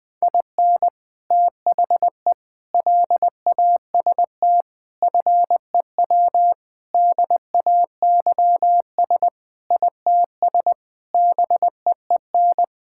Morse code, heard as a keyed tone, about 20 wpm, 700 Hz. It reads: IN THE LAST FEW DAYS ITS BEEN
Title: Sentences with 3-Word Phrases - Spoken Beginning - 20wpm